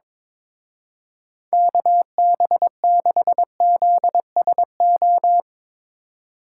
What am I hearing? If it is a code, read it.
Morse code, heard as a keyed tone, about 22 wpm, 700 Hz. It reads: KB6ZSO
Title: Speed Racing - USA 2x3 Call Signs 22wpm